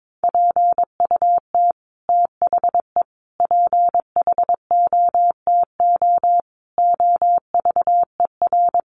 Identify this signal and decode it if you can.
Morse code, heard as a keyed tone, about 22 wpm, 700 Hz. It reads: PUT THE PHOTO OVER
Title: Sentences from Top 1200 Words 22wpm